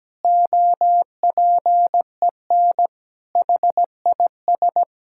Morse code, heard as a keyed tone, about 17 wpm, 700 Hz. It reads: OPEN HIS
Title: Sentences from Top 1600 Words 17wpm